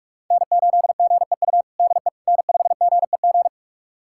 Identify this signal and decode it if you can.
Morse code, heard as a keyed tone, about 46 wpm, 700 Hz. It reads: N8GEU DE N5GEG
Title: USA Callsign DE Callsign - 5 Characters - 45wpm